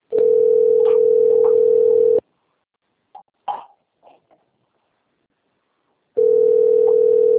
Ringback tone (North America), 440 + 480 Hz.